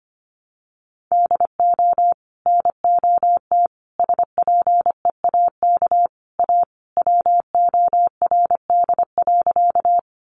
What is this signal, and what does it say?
Morse code, heard as a keyed tone, about 25 wpm, 700 Hz. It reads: DO NOT SPEAK A WORD.
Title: Sentences from Top 1100 Words 25wpm